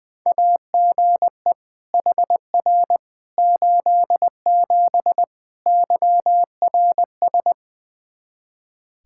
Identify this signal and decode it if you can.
Morse code, heard as a keyed tone, about 20 wpm, 700 Hz. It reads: AGE HR 87 YRS
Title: Practice QSO - Exchange 3 - More Info - 20wpm